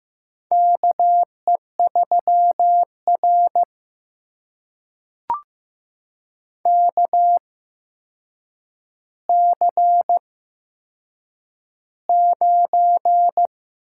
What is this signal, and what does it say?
Morse code, heard as a keyed tone, about 15 wpm, spaced as for 7 wpm, 700 Hz. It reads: KE3R KC9
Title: USA Call Signs - 8x Character Spacing - 15wpm